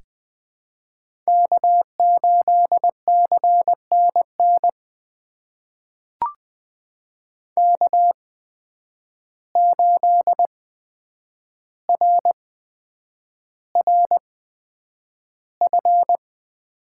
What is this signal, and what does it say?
Morse code, heard as a keyed tone, about 20 wpm, spaced as for 9 wpm, 700 Hz. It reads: K8CNN K8RRF
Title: USA Call Signs - 8x Character Spacing - 20wpm